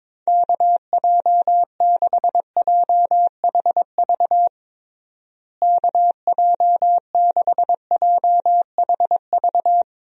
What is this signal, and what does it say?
Morse code, heard as a keyed tone, about 22 wpm, 700 Hz. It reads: KJ6JHV KJ6JHV
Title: USA Call Signs - 6 Characters - Repeated 2x - 22wpm